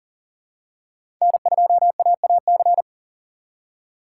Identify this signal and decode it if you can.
Morse code, heard as a keyed tone, about 40 wpm, 690 Hz. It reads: N2AAC
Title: USA Call Signs - 4x Character Spacing - 40wpm